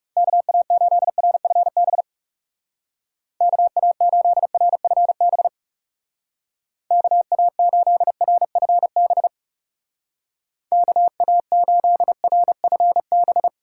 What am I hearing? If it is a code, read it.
Morse code, heard as a keyed tone, about 38 wpm, 700 Hz. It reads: KA8RFB KA8RFB KA8RFB KA8RFB
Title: Speed Racing - USA 2x3 Call Signs 30wpm